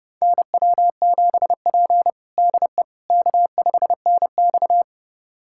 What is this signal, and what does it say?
Morse code, heard as a keyed tone, about 32 wpm, 700 Hz. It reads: NW7P DE K5NX
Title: USA Callsign DE Callsign - 4 Characters - 30wpm